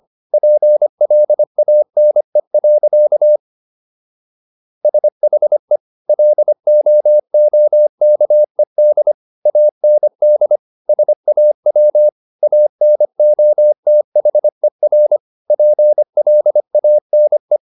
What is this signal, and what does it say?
Morse code, heard as a keyed tone, about 25 wpm, 600 Hz. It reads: PLANE. SHE LOOKED AND SAW ANOTHER PLANE